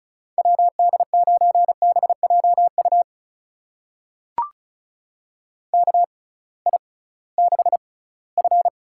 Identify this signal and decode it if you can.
Morse code, heard as a keyed tone, about 36 wpm, spaced as for 20 wpm, 700 Hz. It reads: WD9BJU KI6F
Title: USA Call Signs - 6 Characters - 6x Character Spacing - 35wpm